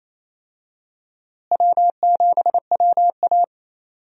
Morse code, heard as a keyed tone, about 28 wpm, 700 Hz. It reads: W7WA
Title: USA Call Signs - 4 Characters - Repeated 3x - 28wpm